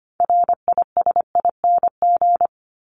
Morse code, heard as a keyed tone, about 25 wpm, 700 Hz. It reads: RISING
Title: ING Suffix - Top 100 Words with Base Words 25wpm